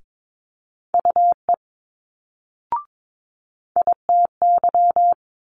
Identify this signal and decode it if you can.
Morse code, heard as a keyed tone, about 22 wpm, 700 Hz. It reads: UE ITY
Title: Multi-letter Phonemes Quick Fire 22wpm